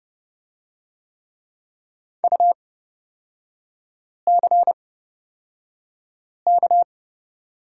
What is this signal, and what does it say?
Morse code, heard as a keyed tone, about 31 wpm, spaced as for 5 wpm, 700 Hz. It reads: UCK
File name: Lesson 36e - BT - Three Characters 30wpm